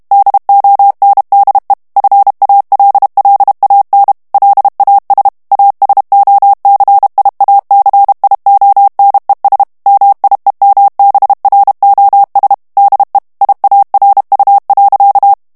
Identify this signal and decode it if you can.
Morse code, heard as a keyed tone, about 32 wpm, 800 Hz. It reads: DONDE FALLAN LAS ASOCIACIONES MIEMBROS DE IARU.
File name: morse mp3 / ESPAGNOL TEXTE EN CLAIR 1 VITESSE 30 MOTS MINUTES